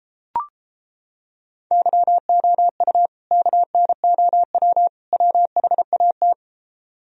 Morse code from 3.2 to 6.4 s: KNOW WHAT

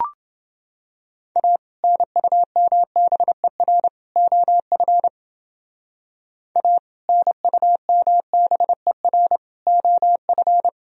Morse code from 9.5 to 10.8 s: OF

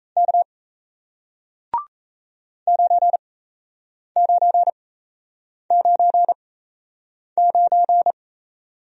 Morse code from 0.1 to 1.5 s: K

